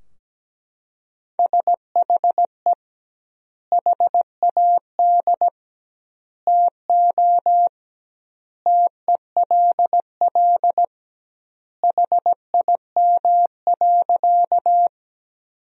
Morse code, heard as a keyed tone, about 17 wpm, 700 Hz. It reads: SHE HAD TO TELL HIM.
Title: 2x Word-Spacing - Sentences from Top 200 Words - 17wpm